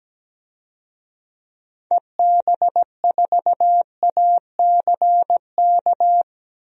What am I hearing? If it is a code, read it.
Morse code, heard as a keyed tone, about 17 wpm, 700 Hz. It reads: EB4ACK
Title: DXCC-100 Call Signs - 6 Characters - 17wpm